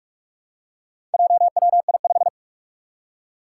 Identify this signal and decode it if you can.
Morse code, heard as a keyed tone, about 46 wpm, 700 Hz. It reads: JWI5
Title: Four Challenging Letter-Number Combinations 45wpm